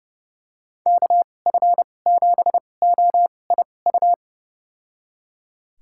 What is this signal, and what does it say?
Morse code, heard as a keyed tone, about 30 wpm, spaced as for 22 wpm, 700 Hz. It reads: KF7OIU
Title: USA Call Signs - 6 Characters - 2x Character Spacing - 30wpm